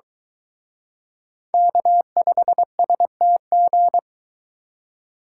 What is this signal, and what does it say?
Morse code, heard as a keyed tone, about 23 wpm, 700 Hz. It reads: K5STG